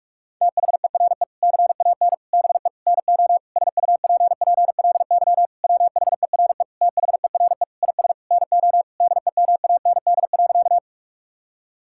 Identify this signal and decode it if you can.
Morse code, heard as a keyed tone, about 46 wpm, 700 Hz. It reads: THERE CAN BE NO SUPPLY WHERE THERE IS NO DEMAND.
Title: Sentences from Top 1000 Words - Review Part 5 - 45wpm